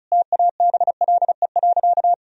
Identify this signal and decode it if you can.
Morse code, heard as a keyed tone, about 36 wpm, 700 Hz. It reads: TABLE.
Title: Sentences from Top 1000 Words - Review Part 2 - 35wpm